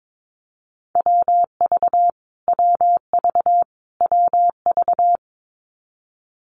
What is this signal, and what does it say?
Morse code, heard as a keyed tone, about 22 wpm, 700 Hz. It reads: WV WV WV
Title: US States Repeated 3x 22wpm